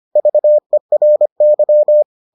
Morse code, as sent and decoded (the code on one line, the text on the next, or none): ...- . .-. -.--
VERY